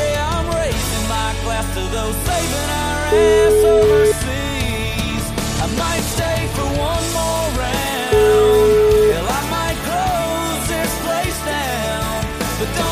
Ringback tone (Europe, 425 Hz), with noise.